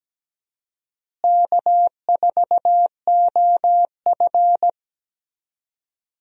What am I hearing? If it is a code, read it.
Morse code, heard as a keyed tone, about 17 wpm, 700 Hz. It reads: K4OF
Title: USA Call Signs - 4 Characters - Repeated 2x - 17wpm